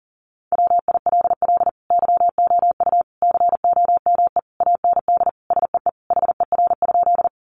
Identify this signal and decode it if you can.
Morse code, heard as a keyed tone, about 40 wpm, 700 Hz. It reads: WILL YOU COME AND SEE HER?
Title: Sentences from Top 100 Words 40wpm